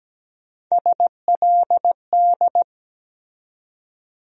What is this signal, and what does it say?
Morse code, heard as a keyed tone, about 17 wpm, 700 Hz. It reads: SLD
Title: USA Call Sign Postfixes - 3 Letters 17wpm